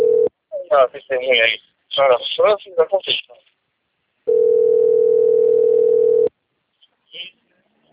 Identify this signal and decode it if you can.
Ringback tone (North America), 440 + 480 Hz, with speech.
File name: out- - - - - .38127